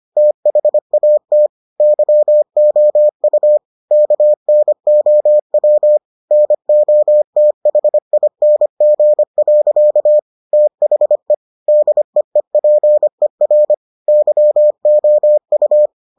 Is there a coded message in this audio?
Yes: Morse code from 0.2 to 15.9 s, heard as a keyed tone, about 25 wpm, 600 Hz. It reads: THAT YOU KNOW NOTHING. THE DEEPER YOU